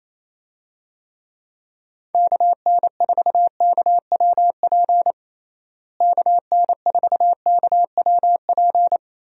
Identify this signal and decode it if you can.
Morse code, heard as a keyed tone, about 28 wpm, 700 Hz. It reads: KN4KWP KN4KWP
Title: USA Call Signs - 6 Characters - Repeated 2x - 28wpm